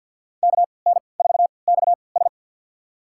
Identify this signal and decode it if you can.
Morse code, heard as a keyed tone, about 53 wpm, spaced as for 29 wpm, 700 Hz. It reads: KN4XS
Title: USA Call Signs - 5 Characters - 3x Character Spacing - 50wpm